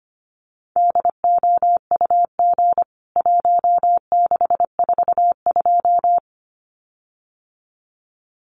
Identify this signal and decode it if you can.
Morse code, heard as a keyed tone, about 25 wpm, 700 Hz. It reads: DOUG 1642
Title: CWT Exchange 25wpm